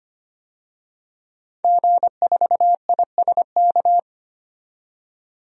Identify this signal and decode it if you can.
Morse code, heard as a keyed tone, about 25 wpm, 700 Hz. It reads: G4ISK
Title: DXCC-100 Call Signs - 5 Characters - 25wpm